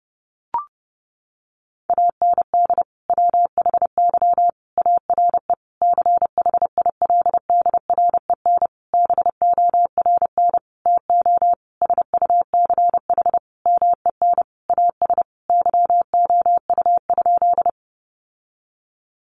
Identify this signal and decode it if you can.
Morse code, heard as a keyed tone, about 30 wpm, 700 Hz. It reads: AND WHY ARE CHILDREN BORN TO SUCH MEN AS YOU?
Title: Sentences from Top 1000 Words 30wpm